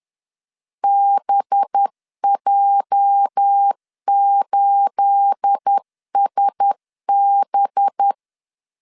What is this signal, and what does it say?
Morse code, heard as a keyed tone, about 10 wpm, 800 Hz. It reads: BJ8SB
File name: LEZIONI / 9 28 10wpm 2842